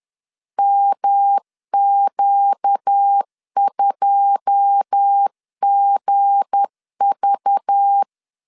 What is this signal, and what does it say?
Morse code, heard as a keyed tone, about 10 wpm, 800 Hz. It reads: MQ2GV
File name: CW / DET / 1 9 1500C S B 10wpm 4439